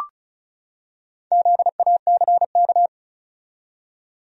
Morse code, heard as a keyed tone, about 34 wpm, 700 Hz. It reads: ZACK
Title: USA Common Names - Max 5 Letters - 35wpm